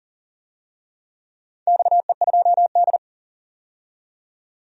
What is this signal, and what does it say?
Morse code, heard as a keyed tone, about 40 wpm, 700 Hz. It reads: XE2D
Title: DXCC-100 Call Signs - 4 Characters - 40wpm